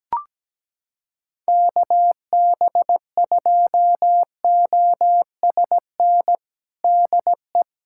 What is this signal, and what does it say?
Morse code, heard as a keyed tone, about 17 wpm, 700 Hz. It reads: KB2OSN DE
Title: USA Callsign DE Callsign 17wpm